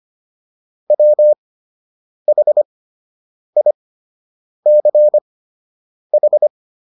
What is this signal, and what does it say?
Morse code, heard as a keyed tone, about 25 wpm, spaced as for 8 wpm, 600 Hz. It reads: WHICH